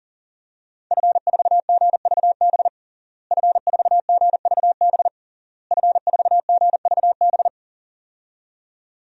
Morse code, heard as a keyed tone, about 37 wpm, 700 Hz. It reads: F4GVB F4GVB F4GVB
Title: DXCC-100 Call Signs - 5 Characters - Repeated 3x - 40wpm